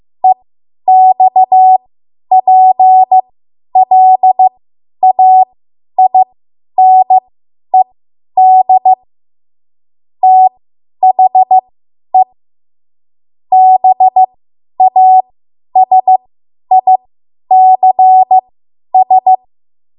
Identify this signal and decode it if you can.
Morse code, heard as a keyed tone, about 15 wpm, spaced as for 10 wpm, 750 Hz. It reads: EXPLAINED THE BASICS
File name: Morse / 10WPM